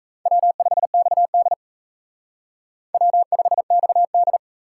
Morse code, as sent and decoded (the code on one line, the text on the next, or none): .-- ..... -..- -.. / .-- ..... -..- -..
W5XD W5XD